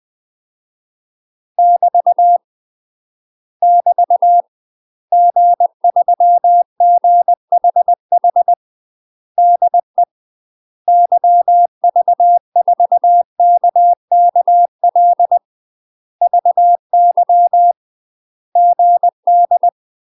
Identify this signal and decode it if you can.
Morse code, heard as a keyed tone, about 20 wpm, 700 Hz. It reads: <BT> <BT> G3GHH DE YV4KKL VY GD
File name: UK MASTER QSOs TRAINING / 20WPM / UK MASTER QSOs 0012